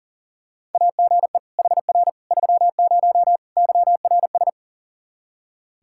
Morse code, heard as a keyed tone, about 40 wpm, 700 Hz. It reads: AGE HR 30 YRS
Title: Practice QSO - Exchange 3 - More Info - 40wpm